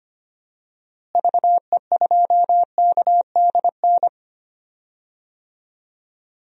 Morse code, heard as a keyed tone, about 25 wpm, 700 Hz. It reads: VE2KDN
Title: DXCC-100 Call Signs - 6 Characters - Repeated 3x - 25wpm